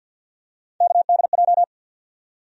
Morse code, heard as a keyed tone, about 52 wpm, 700 Hz. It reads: KDJ
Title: USA Call Sign Postfixes - 3 Letters 50wpm